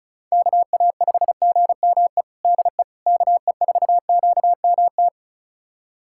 Morse code, heard as a keyed tone, about 36 wpm, 700 Hz. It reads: KA5GME DE KE4QMT
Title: USA Callsign DE Callsign - 6 Characters - 35wpm